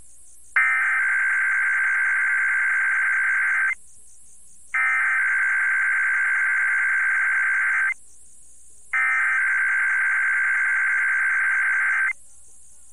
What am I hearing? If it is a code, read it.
Emergency Alert System (SAME) tones (header bursts), heard as harsh buzzing bursts. It is heard from 0.6 to 12.1 s.